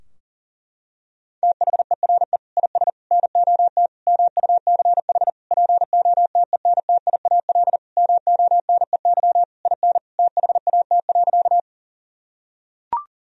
Morse code, heard as a keyed tone, about 40 wpm, 700 Hz. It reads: THERE IS NOT MUCH POTENTIAL MONEY IN THAT.
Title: Sentences from Top 1000 Words - Review Part 4 - 40wpm